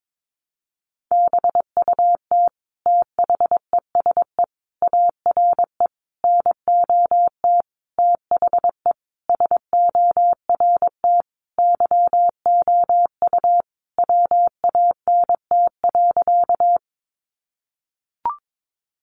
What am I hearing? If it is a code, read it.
Morse code, heard as a keyed tone, about 22 wpm, 700 Hz. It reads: BUT THESE ARE NOT THE SORT YOU WANT.
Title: Sentences from Top 1300 Words 22wpm